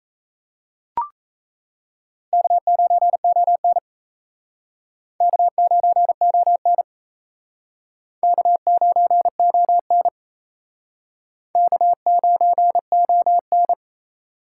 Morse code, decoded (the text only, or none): K9ON K9ON K9ON K9ON